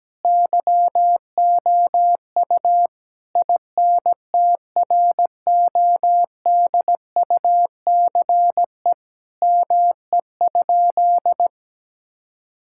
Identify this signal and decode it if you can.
Morse code, heard as a keyed tone, about 17 wpm, 700 Hz. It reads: YOU INTRODUCE ME?
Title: Sentences from Top 1600 Words 17wpm